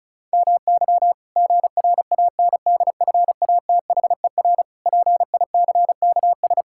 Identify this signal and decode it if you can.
Morse code, heard as a keyed tone, about 36 wpm, 700 Hz. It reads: MY GRANDFATHER PICKS
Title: Pangram Challenge v2 - 35wpm